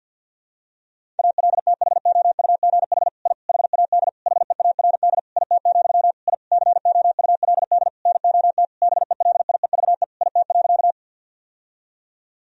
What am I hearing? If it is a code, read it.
Morse code, heard as a keyed tone, about 52 wpm, 700 Hz. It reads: ALTHOUGH I HAD HEARD IT, I COULD NOT BELIEVE IT.
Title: Sentences from Top 1000 Words - Review Part 3 - 50wpm